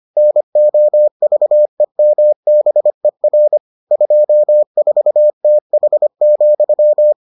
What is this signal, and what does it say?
Morse code, heard as a keyed tone, about 25 wpm, 600 Hz. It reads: NOVEMBER 24TH,